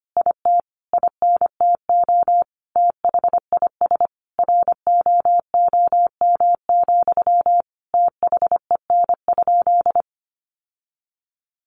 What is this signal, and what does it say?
Morse code, heard as a keyed tone, about 25 wpm, 700 Hz. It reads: IT INTO THIS ROOM, THEN?
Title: Sentences from Top 600 Words 25wpm